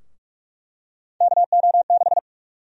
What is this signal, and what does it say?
Morse code, heard as a keyed tone, about 43 wpm, 700 Hz. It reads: KO6
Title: USA Call Sign Prefixes - Repeated 2x - 45wpm